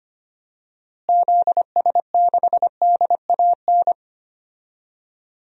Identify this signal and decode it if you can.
Morse code, heard as a keyed tone, about 25 wpm, 700 Hz. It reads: ZS6DAN
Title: DXCC-100 Call Signs - 6 Characters - Repeated 3x - 25wpm